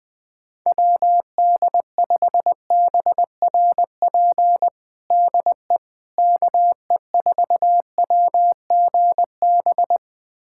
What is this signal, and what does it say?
Morse code, heard as a keyed tone, about 20 wpm, 700 Hz. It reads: WD5BRP DE KE4WGB
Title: USA Callsign DE Callsign 20wpm